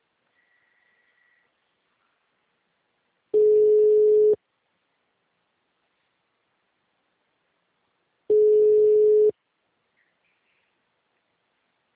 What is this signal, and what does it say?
Ringback tone (Europe, 425 Hz).
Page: Aracatuba